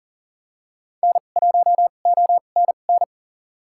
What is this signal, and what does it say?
Morse code, heard as a keyed tone, about 40 wpm, spaced as for 29 wpm, 700 Hz. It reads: N1ONN
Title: USA Call Signs - 5 Characters - 2x Character Spacing - 40wpm